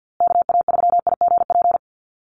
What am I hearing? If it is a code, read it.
Morse code, heard as a keyed tone, about 50 wpm, 700 Hz. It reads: KA3IGP
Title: USA Call Signs Repeated 2x 50wpm